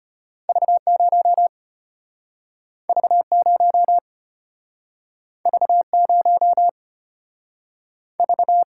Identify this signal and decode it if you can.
Morse code, heard as a keyed tone, about 34 wpm, 700 Hz. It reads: V0 V0 V0 V